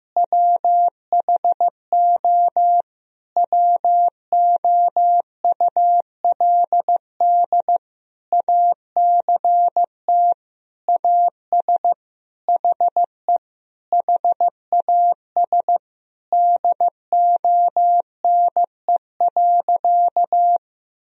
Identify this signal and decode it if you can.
Morse code, heard as a keyed tone, about 16 wpm, 700 Hz. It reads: WHO WOULD ACT AS HE HAS DONE.
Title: Sentences from Top 1000 Words - Review Part 1 - 15wpm